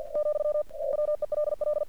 Morse code, heard as a keyed tone, about 49 wpm, 600 Hz. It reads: , WELL
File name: archived / CRSnet